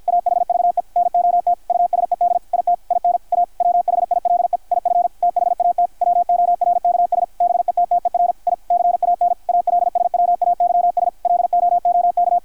Morse code, heard as a keyed tone, about 52 wpm, 660 Hz, with radio noise. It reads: AHVE NOT USED IT IN A WHILE IF THAT WORKS BETTER I CAN ALSWAYS LOOK
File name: archived / CRSnet